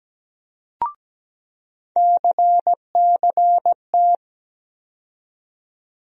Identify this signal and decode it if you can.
Morse code, heard as a keyed tone, about 17 wpm, 700 Hz. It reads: CCT
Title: USA Call Sign Postfixes - 3 Letters 17wpm